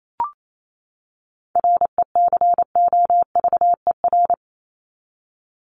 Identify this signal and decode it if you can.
Morse code, heard as a keyed tone, about 28 wpm, 700 Hz. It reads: RECOVER
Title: Top 1751-2000 Words 28wpm